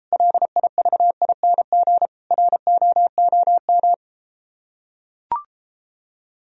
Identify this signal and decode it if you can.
Morse code, heard as a keyed tone, about 33 wpm, 700 Hz. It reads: LIVING ROOM